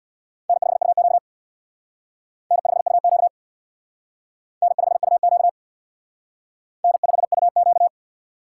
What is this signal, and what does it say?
Morse code, heard as a keyed tone, about 64 wpm, 700 Hz. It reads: N5UQ N5UQ N5UQ N5UQ